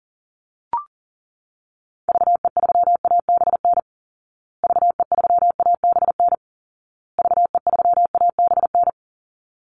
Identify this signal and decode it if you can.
Morse code, heard as a keyed tone, about 40 wpm, 700 Hz. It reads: VE3ABN VE3ABN VE3ABN
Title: Call Signs Repeated 2x - 40wpm